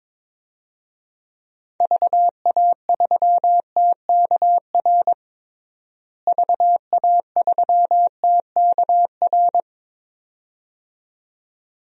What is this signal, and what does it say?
Morse code, heard as a keyed tone, about 21 wpm, 700 Hz. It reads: VA3TKR VA3TKR
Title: DXCC-100 Call Signs - 6 Characters - Repeated 2x - 22wpm